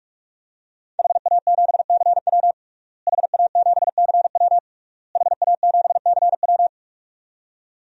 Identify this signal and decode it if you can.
Morse code, heard as a keyed tone, about 46 wpm, 700 Hz. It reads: HA7CW HA7CW HA7CW
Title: DXCC-100 Call Signs - 5 Characters - Repeated 3x - 45wpm